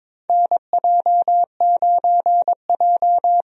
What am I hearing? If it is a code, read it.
Morse code, heard as a keyed tone, about 22 wpm, 700 Hz. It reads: NJ9J
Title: USA Call Signs - 4 Characters - Repeated 2x - 22wpm